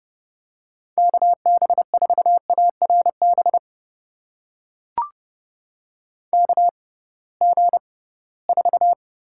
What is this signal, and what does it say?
Morse code, heard as a keyed tone, about 30 wpm, spaced as for 19 wpm, 700 Hz. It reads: KB4ARB KG4
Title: USA Call Signs - 6 Characters - 6x Character Spacing - 30wpm